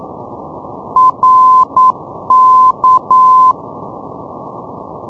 Morse code, heard as a keyed tone, about 9 wpm, 1030 Hz. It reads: RK
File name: europe / isl / isl-620 ndb reykjavik rk-355-np-isl-2012-04-k-rb